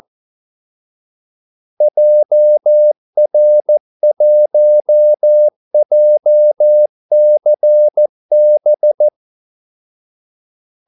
Morse code, heard as a keyed tone, about 14 wpm, 600 Hz. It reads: JR1JCB
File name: Callsigns-14wpm